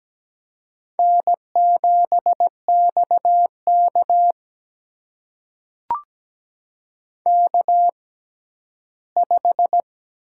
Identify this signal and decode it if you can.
Morse code, heard as a keyed tone, about 17 wpm, spaced as for 11 wpm, 700 Hz. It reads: N7XK K5
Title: USA Call Signs - 4 Characters - 6x Character Spacing - 17wpm